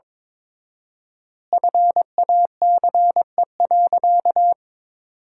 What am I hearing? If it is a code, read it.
Morse code, heard as a keyed tone, about 22 wpm, 700 Hz. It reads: FACE.
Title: 4x Word-Spacing - Sentences from Top 300 Words - 22wpm